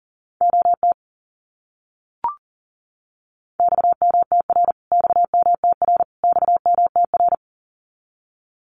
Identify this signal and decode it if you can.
Morse code, heard as a keyed tone, about 40 wpm, 700 Hz. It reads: OT XMTR XMTR XMTR
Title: Abbreviations Repeated 3x - 40wpm